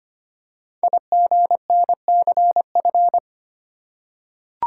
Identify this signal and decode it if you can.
Morse code, heard as a keyed tone, about 25 wpm, 700 Hz. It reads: IGNCF
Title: Code Groups - 5 Letters - 25wpm